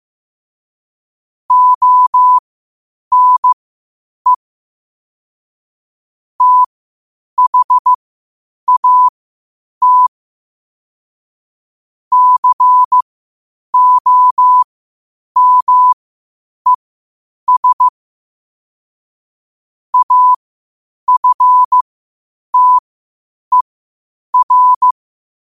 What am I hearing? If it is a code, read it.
Morse code, heard as a keyed tone, about 15 wpm, spaced as for 8 wpm, 1000 Hz. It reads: ONE THAT COMES AFTER
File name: qotdfiles / 8wpm